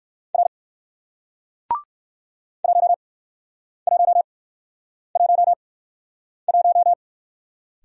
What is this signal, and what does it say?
Morse code, heard as a keyed tone, about 57 wpm, spaced as for 9 wpm, 700 Hz. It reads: R 1111